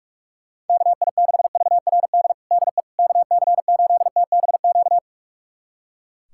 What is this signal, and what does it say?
Morse code, heard as a keyed tone, about 45 wpm, 700 Hz. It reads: KI6VRD DE KC8TBQ
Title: USA Callsign DE Callsign 45wpm